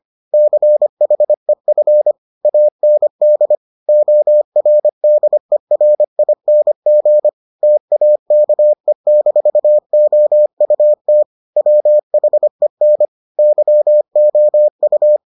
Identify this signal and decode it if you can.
Morse code, heard as a keyed tone, about 25 wpm, 600 Hz. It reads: CHEF AND ORDERING TAKE-OUT WHEN YOU